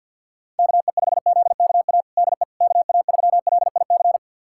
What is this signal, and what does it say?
Morse code, heard as a keyed tone, about 50 wpm, 700 Hz. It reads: KE5ZKA DE KA3LIC
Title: USA Callsign DE Callsign 50wpm